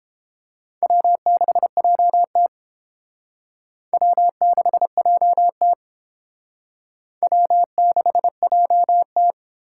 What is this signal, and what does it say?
Morse code, heard as a keyed tone, about 29 wpm, 700 Hz. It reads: W6JT W6JT W6JT